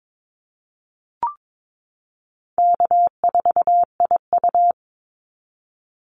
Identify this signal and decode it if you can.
Morse code, heard as a keyed tone, about 22 wpm, 700 Hz. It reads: K4IU
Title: Easy Call Signs Repeated 3x - 22wpm